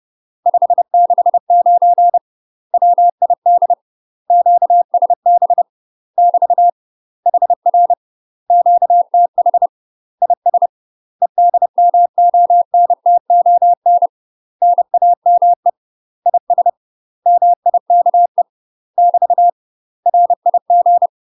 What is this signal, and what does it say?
Morse code, heard as a keyed tone, about 30 wpm, 700 Hz. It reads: 569 WID QSB <BT> HR QTH IS EDMONTON NAME IS MIKE <BT> RIG